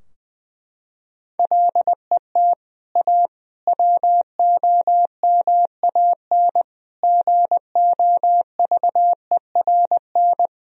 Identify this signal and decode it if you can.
Morse code, heard as a keyed tone, about 20 wpm, 700 Hz. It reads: LET A WOMAN GOVERN
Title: Sentences from Top 1000 Words - Review Part 2 - 20wpm